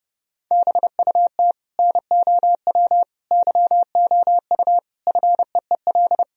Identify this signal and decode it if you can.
Morse code, heard as a keyed tone, about 30 wpm, 700 Hz. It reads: BUT NOW YOU FEEL
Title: Sentences from Top 1000 Words - Review Part 1 - 30wpm